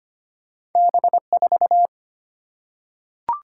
Morse code, heard as a keyed tone, about 25 wpm, 690 Hz. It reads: B4
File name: Two Letter-Number Combinations - 25wpm